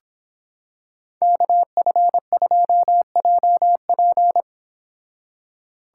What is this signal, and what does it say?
Morse code, heard as a keyed tone, about 26 wpm, 700 Hz. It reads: KF2JP